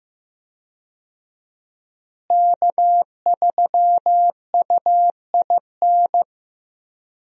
Morse code, heard as a keyed tone, about 15 wpm, 700 Hz. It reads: K3UIN
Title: USA Call Signs - 5 Characters - Repeated 3x - 15wpm